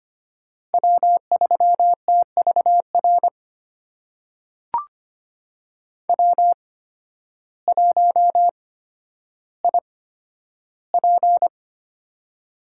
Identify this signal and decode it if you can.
Morse code, heard as a keyed tone, about 25 wpm, spaced as for 12 wpm, 700 Hz. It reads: W3TVR W1IP
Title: USA Call Signs - 8x Character Spacing - 25wpm